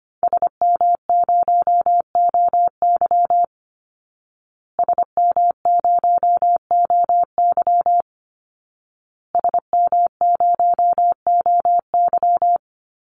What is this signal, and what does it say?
Morse code, heard as a keyed tone, about 25 wpm, 700 Hz. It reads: SM0OY SM0OY SM0OY
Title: Call Signs Repeated 2x - 25wpm